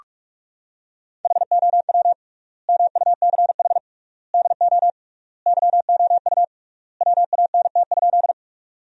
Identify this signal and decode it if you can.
Morse code, heard as a keyed tone, about 46 wpm, 700 Hz. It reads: HOW MUCH DO YOU WANT?
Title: 3x Word-Spacing - Sentences from Top 200 Words - 45wpm